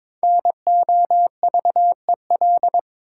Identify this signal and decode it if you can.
Morse code, heard as a keyed tone, about 22 wpm, 700 Hz. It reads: NOVEL
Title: Sentences from Top 1300 Words - New Word Drill Alt - 22wpm